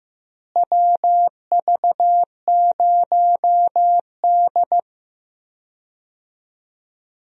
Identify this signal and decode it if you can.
Morse code, heard as a keyed tone, about 16 wpm, 700 Hz. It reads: WV0D
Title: USA Call Signs - 4 Characters - Repeated 3x - 15wpm